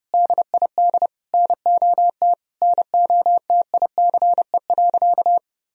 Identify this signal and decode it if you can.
Morse code, heard as a keyed tone, about 32 wpm, 700 Hz. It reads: DID NOT NOTICE.
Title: Sentences from Top 1000 Words - Review Part 2 - 30wpm